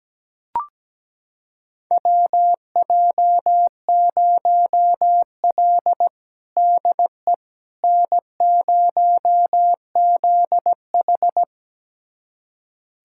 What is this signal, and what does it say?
Morse code, heard as a keyed tone, about 17 wpm, 700 Hz. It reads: WJ0L DE N0ZH
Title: USA Callsign DE Callsign - 4 Characters - 17wpm